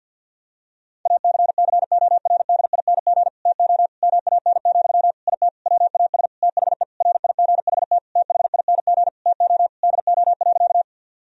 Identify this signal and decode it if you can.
Morse code, heard as a keyed tone, about 50 wpm, 700 Hz. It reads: ACCORDING TO MAN, IT WAS THE RIGHT THING TO DO.
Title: Sentences from Top 1000 Words - Review Part 4 - 50wpm